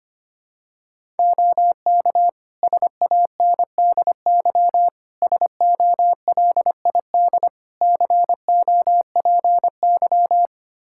Morse code, heard as a keyed tone, about 25 wpm, 700 Hz. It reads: OK SANDY SOLID COPY
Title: Practice QSO - Exchange 3 - More Info - 25wpm